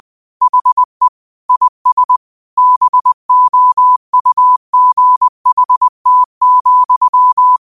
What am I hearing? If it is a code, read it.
Morse code, heard as a keyed tone, about 20 wpm, 1000 Hz. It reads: HE IS BOUGHT,
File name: qotdfiles / 20wpm